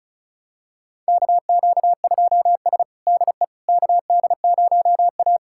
Morse code, heard as a keyed tone, about 36 wpm, 700 Hz. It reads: KQ2S DE KD0A
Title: USA Callsign DE Callsign - 4 Characters - 35wpm